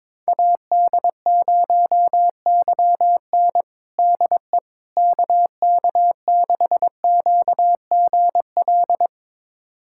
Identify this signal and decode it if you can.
Morse code, heard as a keyed tone, about 21 wpm, 700 Hz. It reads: AD0YN DE KK6QGL
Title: USA Callsign DE Callsign 22wpm